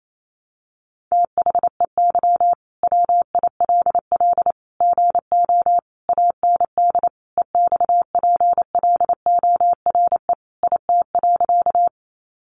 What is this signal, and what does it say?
Morse code, heard as a keyed tone, about 28 wpm, 700 Hz. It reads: THEY WILL GO AND EXPLORE IT.
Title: Sentences from Top 1100 Words 28wpm